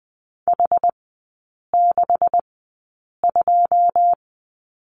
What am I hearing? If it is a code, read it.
Morse code, heard as a keyed tone, about 20 wpm, spaced as for 8 wpm, 690 Hz. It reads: H62
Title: 133 Lesson 22e - Question Mark - Three Characters 20wpm